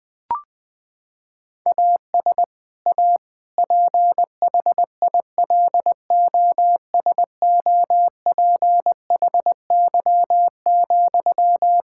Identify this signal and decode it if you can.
Morse code, heard as a keyed tone, about 20 wpm, 700 Hz. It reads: AS A PHILOSOPHY,
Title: Sentences from Top 1000 Words - Review Part 6 - 20wpm